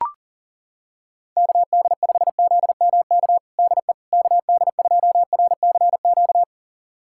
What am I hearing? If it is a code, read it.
Morse code, heard as a keyed tone, about 40 wpm, 700 Hz. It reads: KD5ZMK DE KD2RCQ
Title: USA Callsign DE Callsign - 6 Characters - 40wpm